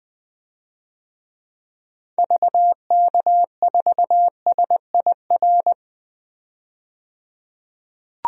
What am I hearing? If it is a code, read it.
Morse code, heard as a keyed tone, about 20 wpm, 700 Hz. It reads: VK4SIR
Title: DXCC-100 Call Signs - 6 Characters - 20wpm